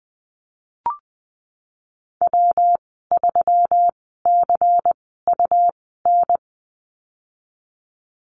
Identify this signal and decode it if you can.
Morse code, heard as a keyed tone, about 20 wpm, spaced as for 15 wpm, 700 Hz. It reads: W3CUN
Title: 193 Lesson 31g - Letter v - Call Signs 20wpm